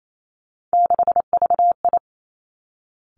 Morse code, heard as a keyed tone, about 28 wpm, 700 Hz. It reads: 6VI
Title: Three Challenging Letter-Number Combinations 28wpm